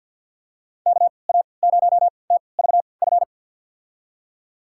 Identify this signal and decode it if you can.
Morse code, heard as a keyed tone, about 51 wpm, spaced as for 28 wpm, 700 Hz. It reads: KA0TVF
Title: USA Call Signs - 3x Character Spacing - 50wpm